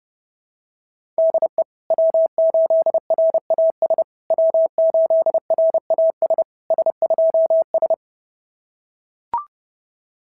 Morse code, heard as a keyed tone, about 30 wpm, 650 Hz. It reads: DE W8RAS W8RAS S2S
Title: SOTA QSO Practice 30wpm